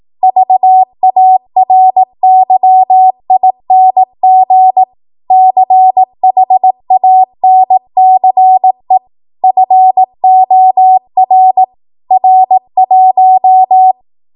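Morse code, heard as a keyed tone, about 18 wpm, 750 Hz. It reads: VARYING CHANCE FOR R1
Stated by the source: Code practice at 18 WPM, a transition run bridging 15 to 20 WPM.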